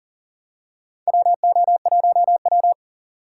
Morse code, heard as a keyed tone, about 37 wpm, 700 Hz. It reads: WO1W
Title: USA Call Signs - 3x Character Spacing - 40wpm